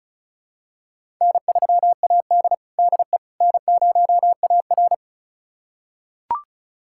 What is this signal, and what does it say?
Morse code, heard as a keyed tone, about 36 wpm, 700 Hz. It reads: N3AD DE N0AR
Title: USA Callsign DE Callsign - 4 Characters - 35wpm